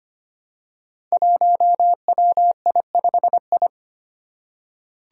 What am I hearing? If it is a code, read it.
Morse code, heard as a keyed tone, about 25 wpm, 700 Hz. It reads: 1WI5I
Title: Five Challenging Letter-Number Combinations 25wpm